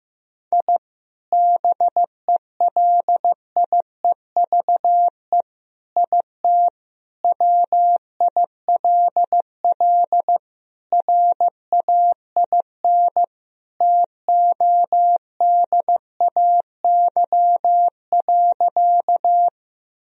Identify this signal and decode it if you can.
Morse code, heard as a keyed tone, about 15 wpm, 700 Hz. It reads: I BELIEVE IT WILL RAIN TODAY.
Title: Sentences from Top 1000 Words - Review Part 1 - 15wpm